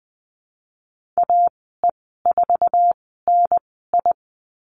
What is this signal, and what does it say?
Morse code, heard as a keyed tone, about 20 wpm, 700 Hz. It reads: A E 4 N I
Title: 055 Lesson 10g - Letter r - Call Signs 20wpm